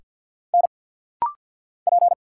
Morse code, heard as a keyed tone, about 50 wpm, 700 Hz. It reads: N P
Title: Single Letter-Number - Rapid-Fire 50wpm